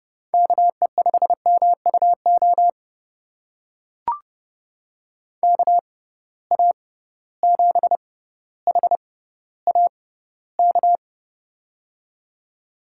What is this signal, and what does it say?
Morse code, heard as a keyed tone, about 30 wpm, spaced as for 15 wpm, 700 Hz. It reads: KE5MUO KA7HAK
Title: USA Call Signs - 6 Characters - 6x Character Spacing - 30wpm